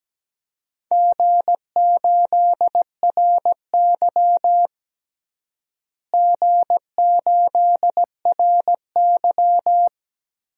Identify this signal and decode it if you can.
Morse code, heard as a keyed tone, about 17 wpm, 700 Hz. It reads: G8RY G8RY
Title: DXCC-100 Call Signs - 4 Characters - Repeated 2x - 17wpm